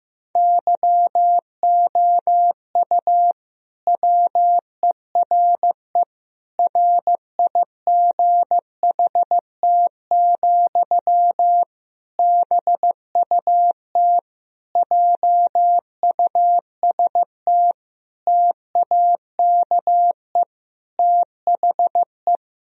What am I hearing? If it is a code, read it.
Morse code, heard as a keyed tone, about 15 wpm, 700 Hz. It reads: YOU WERE RIGHT, BUT JUST TAKE THE
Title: Sentences from Top 1000 Words - Review Part 2 - 15wpm